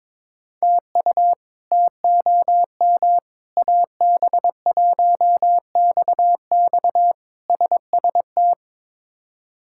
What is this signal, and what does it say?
Morse code, heard as a keyed tone, about 22 wpm, 700 Hz. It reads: TU TOM AB1XX SST
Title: K1USN SST QSO Practice - 22wpm